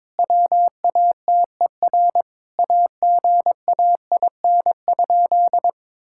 Morse code, heard as a keyed tone, about 21 wpm, 700 Hz. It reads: WATER AGAIN?
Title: Sentences from Top 1600 Words 22wpm